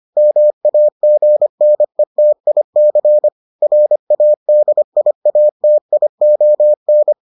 Morse code, heard as a keyed tone, about 25 wpm, 600 Hz. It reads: MAGNETIC RADIATION